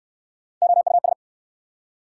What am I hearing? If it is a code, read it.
Morse code, heard as a keyed tone, about 65 wpm, 690 Hz. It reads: OUS